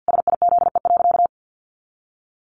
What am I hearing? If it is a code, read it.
Morse code, heard as a keyed tone, about 50 wpm, 700 Hz. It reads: SIZE.
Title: Sentences from Top 400 Words 50wpm